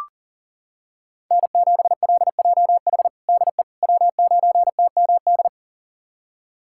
Morse code, heard as a keyed tone, about 40 wpm, 700 Hz. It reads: N7LJH DE W9TMD
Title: USA Callsign DE Callsign - 5 Characters - 40wpm